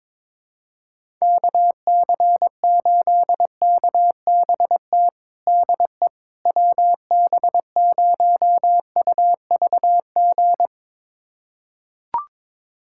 Morse code, heard as a keyed tone, about 22 wpm, 700 Hz. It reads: KC8KBT DE WB0UVG
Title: USA Callsign DE Callsign - 6 Characters - 22wpm